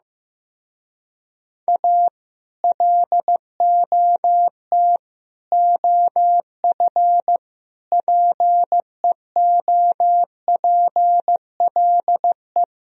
Morse code, heard as a keyed tone, about 15 wpm, 690 Hz. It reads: A LOT OF PEOPLE
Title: Speed-Racing - Sets of 5 Words 15wpm